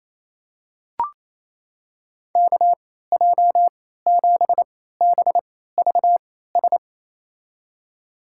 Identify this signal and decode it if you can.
Morse code, heard as a keyed tone, about 28 wpm, spaced as for 16 wpm, 700 Hz. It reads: KJ7BVS
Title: USA Call Signs - 6 Characters - 3x Character Spacing - 28wpm